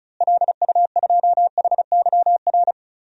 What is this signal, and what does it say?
Morse code, heard as a keyed tone, about 36 wpm, 700 Hz. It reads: LU2HYR
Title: DXCC-100 Call Signs - 6 Characters - Repeated 3x - 35wpm